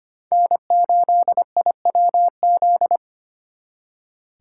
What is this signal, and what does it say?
Morse code, heard as a keyed tone, about 31 wpm, 700 Hz. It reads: N8IWZ
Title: Speed Racing - USA 1x3 Call Signs 25wpm